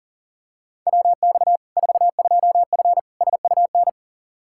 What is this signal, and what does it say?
Morse code, heard as a keyed tone, about 41 wpm, 700 Hz. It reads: WX 42F SUN
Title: K1USN SST QSO Practice 40wpm